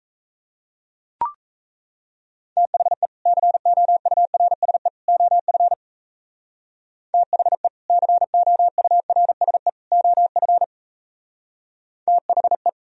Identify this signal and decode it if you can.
Morse code, heard as a keyed tone, about 34 wpm, 700 Hz. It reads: THE COURSE OF THE COURSE OF THE
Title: Speed-Racing - Sets of 3 Words 28wpm